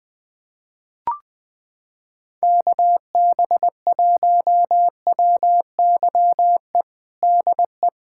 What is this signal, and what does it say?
Morse code, heard as a keyed tone, about 20 wpm, 700 Hz. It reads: KB1WYE DE
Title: USA Callsign DE Callsign - 6 Characters - 20wpm